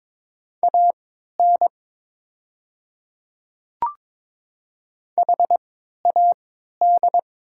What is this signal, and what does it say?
Morse code, heard as a keyed tone, about 22 wpm, spaced as for 13 wpm, 700 Hz. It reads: AN HAD
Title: 3x Character Spacing - Top 100 Words - 22wpm